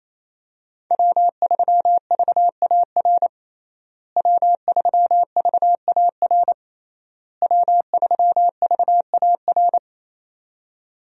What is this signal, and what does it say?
Morse code, heard as a keyed tone, about 28 wpm, 700 Hz. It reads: W3VAR W3VAR W3VAR
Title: USA Call Signs - 5 Characters - Repeated 3x - 28wpm